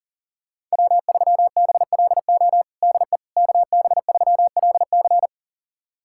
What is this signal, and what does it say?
Morse code, heard as a keyed tone, about 40 wpm, 700 Hz. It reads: W3BLO DE KB3LC
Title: USA Callsign DE Callsign - 5 Characters - 40wpm